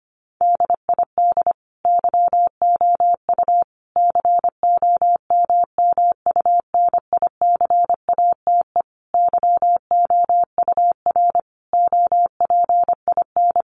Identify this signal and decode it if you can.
Morse code, heard as a keyed tone, about 25 wpm, 700 Hz. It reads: DID YOU COMMUNICATE YOUR OPIN
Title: Sentences from Top 1300 Words 25wpm